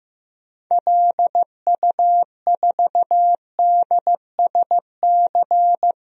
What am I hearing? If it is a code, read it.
Morse code, heard as a keyed tone, about 15 wpm, 700 Hz. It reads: LU4DSC
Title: DXCC-100 Call Signs - 6 Characters - 15wpm